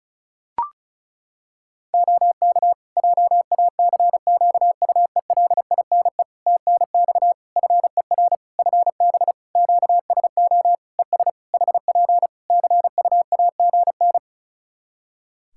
Morse code, heard as a keyed tone, about 36 wpm, 700 Hz. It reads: OK JACQUELINE TNX FER FB QSO ES HP CUAGN
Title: K1USN SST QSO Practice 35wpm